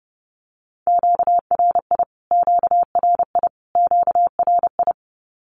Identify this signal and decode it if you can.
Morse code, heard as a keyed tone, about 30 wpm, 700 Hz. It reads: QRI QRI QRI
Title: Q-Codes Repeated 3x 30wpm